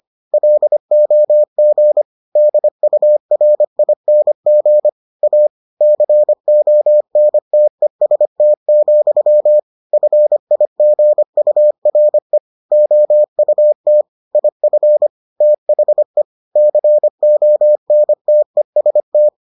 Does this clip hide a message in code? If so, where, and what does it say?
Yes: Morse code from 0.3 to 19.3 s, heard as a keyed tone, about 25 wpm, 600 Hz. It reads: LOG DURING A CONTEST, FIGURE OUT IF THE CONTEST